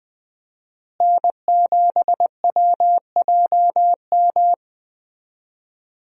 Morse code, heard as a keyed tone, about 20 wpm, 700 Hz. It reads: N7WJM
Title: USA Call Signs - 5 Characters - Repeated 2x - 20wpm